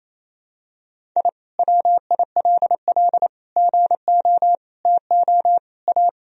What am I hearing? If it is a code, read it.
Morse code, heard as a keyed tone, about 28 wpm, 700 Hz. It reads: I WILL GO TO A
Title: Sentences from Top 1000 Words - Review Part 4 - 28wpm